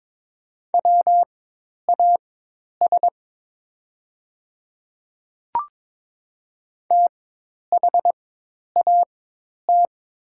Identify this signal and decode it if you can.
Morse code, heard as a keyed tone, about 22 wpm, spaced as for 10 wpm, 700 Hz. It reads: WAS THAT
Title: 4x Character Spacing - Top 100 Words - 22wpm